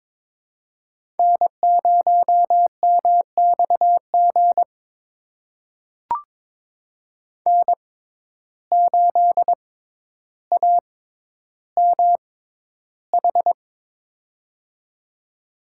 Morse code, heard as a keyed tone, about 22 wpm, spaced as for 11 wpm, 700 Hz. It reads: N0MXG N8AMH
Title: USA Call Signs - 5 Characters - 6x Character Spacing - 22wpm